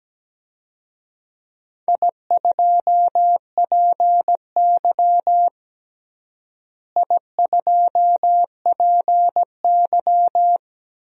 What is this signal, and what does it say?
Morse code, heard as a keyed tone, about 17 wpm, 700 Hz. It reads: I2PY I2PY
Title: DXCC-100 Call Signs - 4 Characters - Repeated 3x - 17wpm